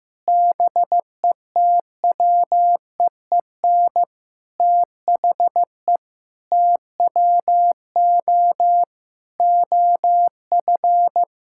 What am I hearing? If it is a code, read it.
Morse code, heard as a keyed tone, about 15 wpm, 700 Hz. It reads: BETWEEN THE TWO OF
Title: Sentences from Top 1000 Words - Review Part 6 - 15wpm